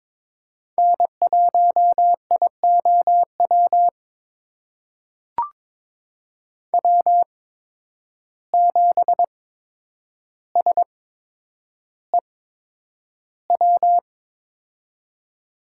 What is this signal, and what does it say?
Morse code, heard as a keyed tone, about 22 wpm, spaced as for 9 wpm, 700 Hz. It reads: N1IOW W7SEW
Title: USA Call Signs - 5 Characters - 8x Character Spacing - 22wpm